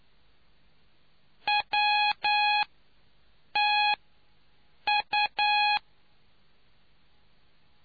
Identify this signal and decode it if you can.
Morse code, heard as a keyed tone, about 9 wpm, spaced as for 6 wpm, 830 Hz. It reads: WTU